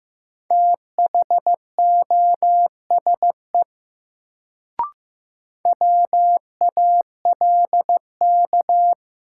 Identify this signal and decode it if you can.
Morse code, heard as a keyed tone, about 15 wpm, 700 Hz. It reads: THOSE WALK
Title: Sentences from Top 300 Words - New Word Drill - 15wpm